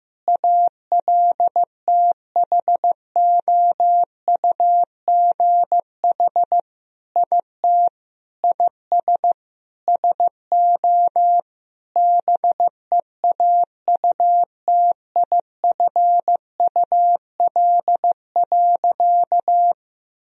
Morse code, heard as a keyed tone, about 15 wpm, 700 Hz. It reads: ALTHOUGH IT IS SO BEAUTIFUL.
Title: Sentences from Top 1000 Words - Review Part 3 - 15wpm